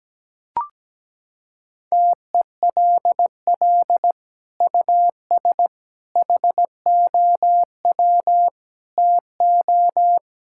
Morse code, heard as a keyed tone, about 17 wpm, 700 Hz. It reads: TELL US HOW TO
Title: Sentences from Top 1600 Words 17wpm